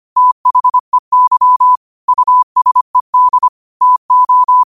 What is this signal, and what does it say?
Morse code, heard as a keyed tone, about 25 wpm, 1000 Hz. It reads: THEY USED TO
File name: qotdfiles / 25wpm